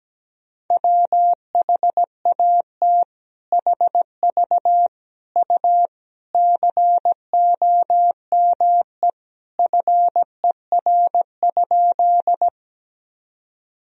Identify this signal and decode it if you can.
Morse code, heard as a keyed tone, about 17 wpm, 700 Hz. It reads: WHAT HV U COME FER?
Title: Sentences from Top 100 Words and Abbreviations 17wpm